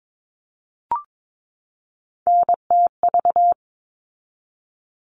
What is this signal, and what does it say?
Morse code, heard as a keyed tone, about 22 wpm, 700 Hz. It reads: NTV
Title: All Cut Numbers - 3-Digit 22wpm